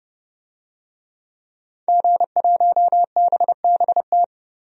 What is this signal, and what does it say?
Morse code, heard as a keyed tone, about 30 wpm, 690 Hz. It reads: G1BBT
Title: DXCC-100 Call Signs - 5 Characters - 30wpm